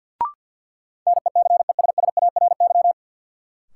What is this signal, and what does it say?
Morse code, heard as a keyed tone, about 53 wpm, 700 Hz. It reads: NECESSARY
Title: Sentences from Top 800 Words - New Word Drill - 50wpm